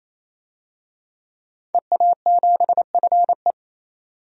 Morse code, heard as a keyed tone, about 28 wpm, 700 Hz. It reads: EA7FE
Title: DXCC-100 Call Signs - 5 Characters - 28wpm